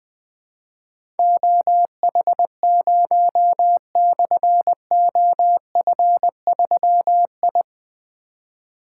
Morse code, heard as a keyed tone, about 20 wpm, 700 Hz. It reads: OH0/OF3I
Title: Difficult Call Signs 20wpm